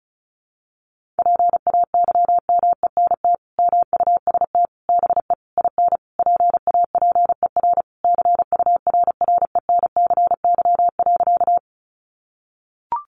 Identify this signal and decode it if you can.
Morse code, heard as a keyed tone, about 34 wpm, 700 Hz. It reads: PAYMENT MUST BE IN PAPER CURRENCY.
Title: Sentences from Top 1500 Words 35wpm